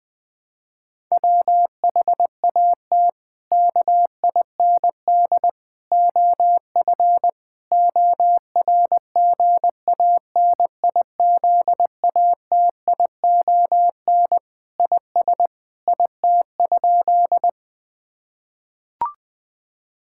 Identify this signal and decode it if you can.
Morse code, heard as a keyed tone, about 20 wpm, 700 Hz. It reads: WHAT KIND OF ORGANIZATION IS IT?
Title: Sentences from Top 1000 Words - Review Part 5 - 20wpm